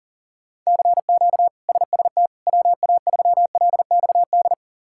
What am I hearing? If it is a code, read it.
Morse code, heard as a keyed tone, about 40 wpm, 700 Hz. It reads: CQ SST WA3LXD
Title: K1USN SST QSO Practice - 40wpm